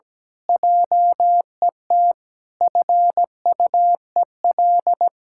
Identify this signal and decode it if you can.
Morse code, heard as a keyed tone, about 17 wpm, 700 Hz. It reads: JET FUEL
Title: Pangram Challenge v2 - 17wpm